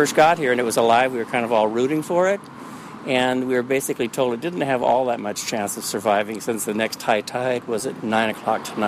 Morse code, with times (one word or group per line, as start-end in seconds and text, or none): none